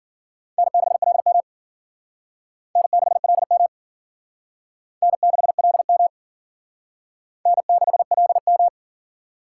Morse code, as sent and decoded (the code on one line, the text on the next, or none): -. -.... .-.. -- / -. -.... .-.. -- / -. -.... .-.. -- / -. -.... .-.. --
N6LM N6LM N6LM N6LM